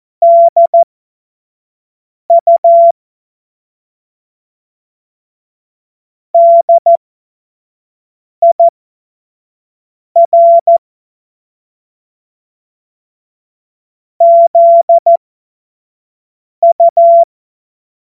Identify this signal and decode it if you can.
Morse code, heard as a keyed tone, about 14 wpm, spaced as for 5 wpm, 680 Hz. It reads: DU DIR ZU
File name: Morsekurs / Witze / Witz 26. Oktober 2021, 25 F70